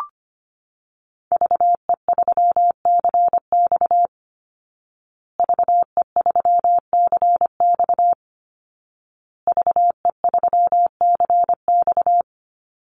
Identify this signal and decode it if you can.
Morse code, heard as a keyed tone, about 25 wpm, 700 Hz. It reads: VE3CX VE3CX VE3CX
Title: Call Signs Repeated 2x - 25wpm